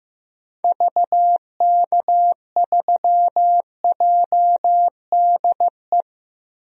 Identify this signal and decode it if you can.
Morse code, heard as a keyed tone, about 15 wpm, 700 Hz. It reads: VK3JDE
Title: DXCC-100 Call Signs - 6 Characters - 15wpm